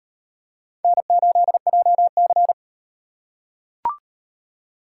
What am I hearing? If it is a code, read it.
Morse code, heard as a keyed tone, about 37 wpm, 700 Hz. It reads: N8JC